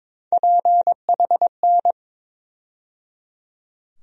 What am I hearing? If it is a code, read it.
Morse code, heard as a keyed tone, about 22 wpm, 700 Hz. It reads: PHN
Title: USA Call Sign Postfixes - 3 Letters 22wpm